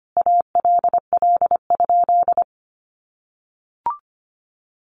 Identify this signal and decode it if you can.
Morse code, heard as a keyed tone, about 25 wpm, 700 Hz. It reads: ALL?
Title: Sentences from Top 1500 Words 25wpm